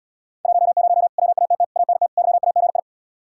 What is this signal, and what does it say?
Morse code, heard as a keyed tone, about 73 wpm, 700 Hz. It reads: 10 WATT ANT YAGI